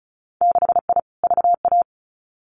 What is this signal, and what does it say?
Morse code, heard as a keyed tone, about 36 wpm, 700 Hz. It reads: 6I VA
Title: Winter Field Day Exchanges Encore 35wpm